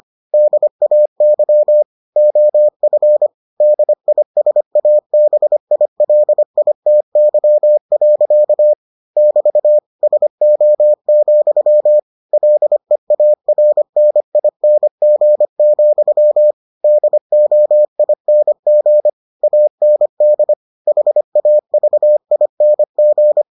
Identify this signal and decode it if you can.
Morse code, heard as a keyed tone, about 25 wpm, 600 Hz. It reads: DAY OF DISABILITY. <BT> SO, LEARNING, DOING AND HAVING